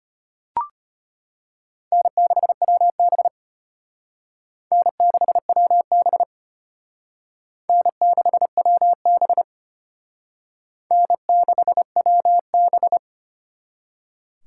Morse code, heard as a keyed tone, about 32 wpm, 700 Hz. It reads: N6WB N6WB N6WB N6WB